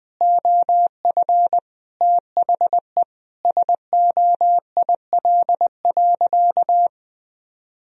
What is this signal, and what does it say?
Morse code, heard as a keyed tone, about 21 wpm, 700 Hz. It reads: OF THE SOIL.
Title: Sentences from Top 1000 Words - Review Part 4 - 20wpm